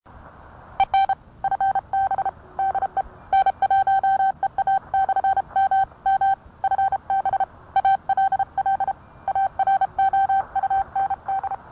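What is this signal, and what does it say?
Morse code, heard as a keyed tone, about 30 wpm, 780 Hz. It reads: EN FB DE N1EA/MM FB ALL AROUND
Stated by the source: Morse code sent on a Vibroplex.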